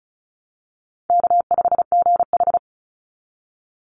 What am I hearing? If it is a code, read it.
Morse code, heard as a keyed tone, about 34 wpm, 700 Hz. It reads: K5GH
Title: Call Signs Repeated 2x - 35wpm